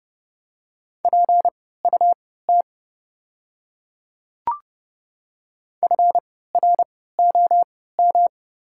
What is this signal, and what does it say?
Morse code, heard as a keyed tone, about 30 wpm, spaced as for 17 wpm, 700 Hz. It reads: PUT FROM
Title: 3x Character Spacing - Top 500 Words - 30wpm